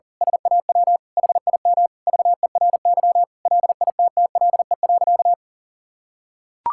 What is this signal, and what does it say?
Morse code, heard as a keyed tone, about 40 wpm, 700 Hz. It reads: SAW HIM VERY LITTLE.
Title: Sentences from Top 1000 Words - Review Part 5 - 40wpm